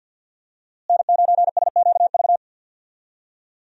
Morse code, heard as a keyed tone, about 48 wpm, 700 Hz. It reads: N9SQV
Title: USA Call Signs - 5 Characters - Repeated 2x - 50wpm